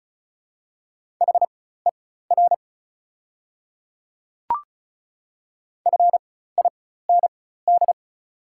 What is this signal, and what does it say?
Morse code, heard as a keyed tone, about 34 wpm, spaced as for 16 wpm, 700 Hz. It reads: HER FIND
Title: 4x Character Spacing - Top 100 Words - 35wpm